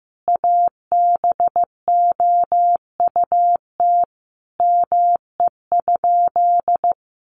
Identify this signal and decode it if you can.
Morse code, heard as a keyed tone, about 15 wpm, 700 Hz. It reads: ABOUT ME?
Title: Sentences from Top 200 Words 15wpm